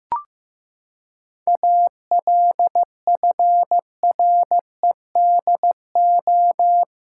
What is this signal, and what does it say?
Morse code, heard as a keyed tone, about 15 wpm, 700 Hz. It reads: ALFREDO
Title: Pangram Challenge v2 - 15wpm